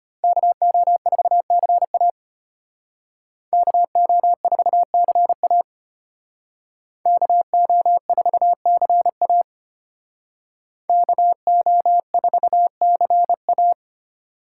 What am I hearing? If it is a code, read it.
Morse code, heard as a keyed tone, about 32 wpm, 700 Hz. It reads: KO4CA KO4CA KO4CA KO4CA